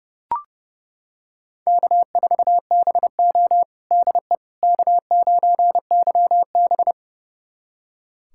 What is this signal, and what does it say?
Morse code, heard as a keyed tone, about 30 wpm, 700 Hz. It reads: K4BO DE K9YB
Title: USA Callsign DE Callsign - 4 Characters - 30wpm